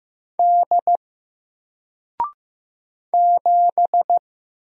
Morse code, heard as a keyed tone, about 15 wpm, 700 Hz. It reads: D 7
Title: Single Letter-Number 15wpm